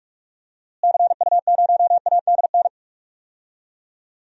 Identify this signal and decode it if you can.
Morse code, heard as a keyed tone, about 46 wpm, 700 Hz. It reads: CU0ADN
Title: DXCC-100 Call Signs - 6 Characters - Repeated 3x - 45wpm